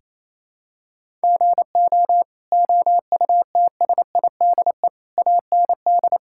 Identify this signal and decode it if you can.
Morse code, heard as a keyed tone, about 28 wpm, 700 Hz. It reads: GO OUTSIDE AND
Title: Sentences from Top 1000 Words - Review Part 6 - 28wpm